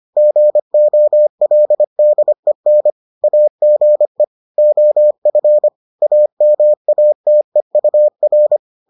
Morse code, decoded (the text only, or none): GOLDEN AGE OF AMATEUR